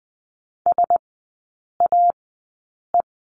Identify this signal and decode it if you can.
Morse code, heard as a keyed tone, about 20 wpm, spaced as for 8 wpm, 700 Hz. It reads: SAE
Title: 138 Lesson 23e - Letter f - Three Characters 20wpm